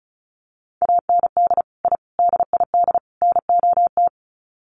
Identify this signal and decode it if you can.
Morse code, heard as a keyed tone, about 34 wpm, 700 Hz. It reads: AND I DID NOT
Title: Sets of 4 Words 35wpm